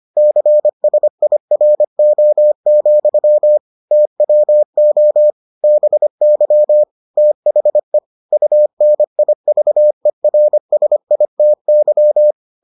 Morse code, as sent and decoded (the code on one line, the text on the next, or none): -.-. ... .. .-. --- --..-- / - .-- --- / -... -.-- / - .... / ..- -. .. ...- . .-. ... .. - -.--
CSIRO, TWO BY THE UNIVERSITY